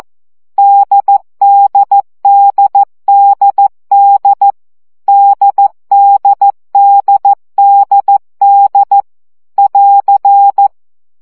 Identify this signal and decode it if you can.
Morse code, heard as a keyed tone, about 14 wpm, 800 Hz. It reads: DDDDD DDDDD <AR>
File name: CW gyakorlas d 12 baud